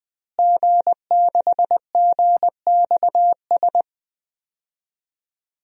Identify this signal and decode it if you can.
Morse code, heard as a keyed tone, about 20 wpm, 700 Hz. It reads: G6GXS
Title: DXCC-100 Call Signs - 5 Characters - Repeated 2x - 20wpm